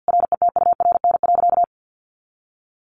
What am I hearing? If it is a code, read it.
Morse code, heard as a keyed tone, about 50 wpm, 700 Hz. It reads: RETURN.
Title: A Sound Of Thunder 50wpm